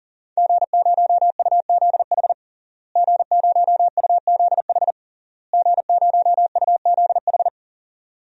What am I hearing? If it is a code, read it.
Morse code, heard as a keyed tone, about 40 wpm, 700 Hz. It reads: G0UZH G0UZH G0UZH
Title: DXCC-100 Call Signs - 5 Characters - Repeated 3x - 40wpm